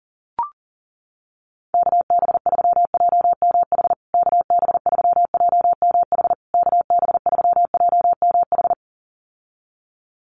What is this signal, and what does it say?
Morse code, heard as a keyed tone, about 40 wpm, 700 Hz. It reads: KB3JMH KB3JMH KB3JMH
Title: USA Call Signs Repeated 3x 40wpm